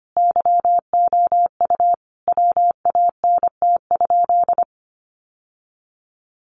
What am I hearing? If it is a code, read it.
Morse code, heard as a keyed tone, about 25 wpm, 700 Hz. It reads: YOU WANT?
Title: Sentences from Top 1200 Words 25wpm